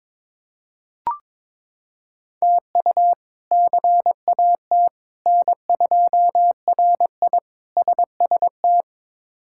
Morse code, heard as a keyed tone, about 22 wpm, 700 Hz. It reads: TU CAT N2RI SST
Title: K1USN SST QSO Practice - 22wpm